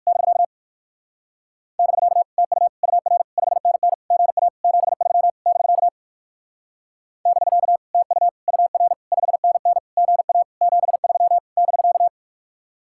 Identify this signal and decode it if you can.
Morse code, heard as a keyed tone, about 57 wpm, 700 Hz. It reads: <BK> <BK> TU UR 5NN GA 73 <BK> <BK> TU UR 5NN GA 73 <BK>